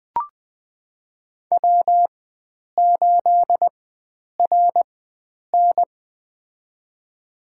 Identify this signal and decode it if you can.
Morse code, heard as a keyed tone, about 21 wpm, spaced as for 9 wpm, 700 Hz. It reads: W8RN
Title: USA Call Signs - 4 Characters - 4x Character Spacing - 20wpm